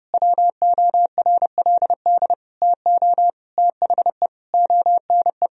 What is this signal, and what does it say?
Morse code, heard as a keyed tone, about 30 wpm, 700 Hz. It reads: WORLD TO THE ONE
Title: Sentences from Top 1000 Words - Review Part 2 - 30wpm